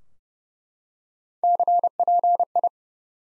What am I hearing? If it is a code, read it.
Morse code, heard as a keyed tone, about 30 wpm, 700 Hz. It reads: CPI
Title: Top 90 Abbr 30WPM